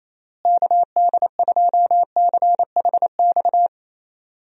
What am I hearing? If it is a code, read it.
Morse code, heard as a keyed tone, about 28 wpm, 700 Hz. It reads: KD2CHX
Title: USA Call Signs - 6 Characters - Repeated 2x - 28wpm